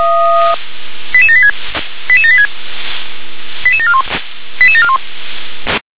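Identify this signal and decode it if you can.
Five-tone selective call (ZVEI), heard as quick runs of five tones, tones of about 70 ms. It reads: 88786 88786 88741 88741
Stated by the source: ZVEI five-tone call received on SDR.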